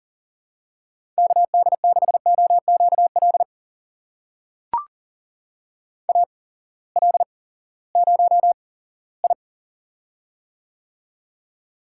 Morse code, heard as a keyed tone, about 37 wpm, spaced as for 19 wpm, 700 Hz. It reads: KD6OQL AL0I
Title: USA Call Signs - 8x Character Spacing - 40wpm